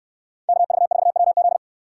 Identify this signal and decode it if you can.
Morse code, heard as a keyed tone, about 69 wpm, 700 Hz. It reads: KV4WZ